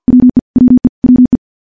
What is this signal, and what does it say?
Morse code, heard as a keyed tone, about 50 wpm, 250 Hz. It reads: WE WE WE
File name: morse-training-50WPM